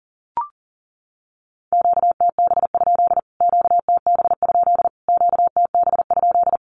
Morse code, heard as a keyed tone, about 40 wpm, 700 Hz. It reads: QTB? QTB? QTB?
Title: Q-Codes Repeated 3x 40wpm